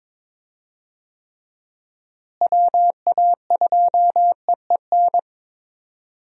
Morse code, heard as a keyed tone, about 21 wpm, 700 Hz. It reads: WA2EEN
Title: USA Call Signs - 6 Characters - Repeated 3x - 22wpm